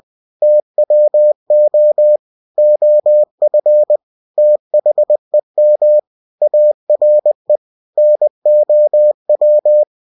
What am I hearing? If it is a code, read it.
Morse code, heard as a keyed tone, about 20 wpm, 600 Hz. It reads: TWO OF THEM ARE NOW
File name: cw-books / winnie-the-pooh / winnie-20wpm-page-0022